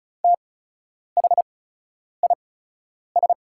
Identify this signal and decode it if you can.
Morse code, heard as a keyed tone, about 35 wpm, spaced as for 10 wpm, 700 Hz. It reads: THIS
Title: 8x Character Spacing - Top 200 Words - 35wpm